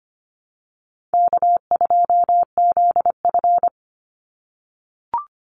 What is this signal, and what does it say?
Morse code, heard as a keyed tone, about 25 wpm, 700 Hz. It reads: K2ZF
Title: Call Signs - 25wpm